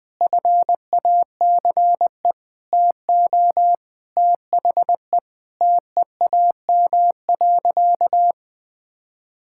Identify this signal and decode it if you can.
Morse code, heard as a keyed tone, about 21 wpm, 700 Hz. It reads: FACE TO THE TEAM.
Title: Sentences with 6-Word Phrases - Spoken Beginning - 20wpm